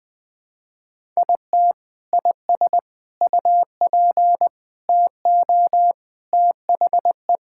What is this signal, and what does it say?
Morse code, heard as a keyed tone, about 20 wpm, 700 Hz. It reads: IT IS UP TO THE
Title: Sentences with 3-Word Phrases - Spoken Beginning - 20wpm